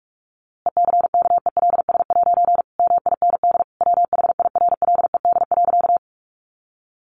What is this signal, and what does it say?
Morse code, heard as a keyed tone, about 44 wpm, 700 Hz. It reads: ECKELS' MIND WHIRLED.
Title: A Sound Of Thunder 45wpm